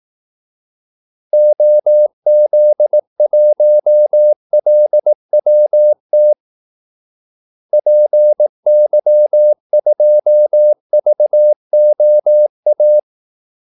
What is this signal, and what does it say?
Morse code, heard as a keyed tone, about 18 wpm, 600 Hz. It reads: OZ1LWT PY2VOA
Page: Kaldesignaler 18 wpm